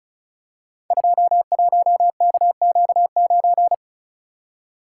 Morse code, heard as a keyed tone, about 36 wpm, 700 Hz. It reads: 21KQ9
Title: Code Groups - 5 Letter:Numbers - 35wpm